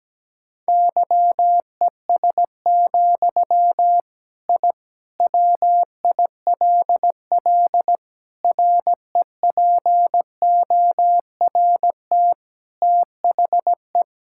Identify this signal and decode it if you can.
Morse code, heard as a keyed tone, about 17 wpm, 700 Hz. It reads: YES, I WILL REPORT THE
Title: Sentences from Top 1000 Words - Review Part 6 - 17wpm